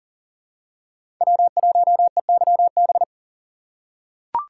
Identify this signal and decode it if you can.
Morse code, heard as a keyed tone, about 40 wpm, 700 Hz. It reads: W1EYB
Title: USA Call Signs - 4x Character Spacing - 40wpm